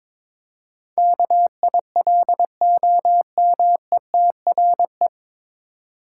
Morse code, heard as a keyed tone, about 22 wpm, 700 Hz. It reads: KILOMETRE
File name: 9 Letter Words - First 3 Letters Given - 22wpm